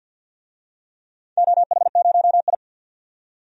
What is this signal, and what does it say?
Morse code, heard as a keyed tone, about 52 wpm, 700 Hz. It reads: OH0I
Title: DXCC-100 Call Signs - 4 Characters - 50wpm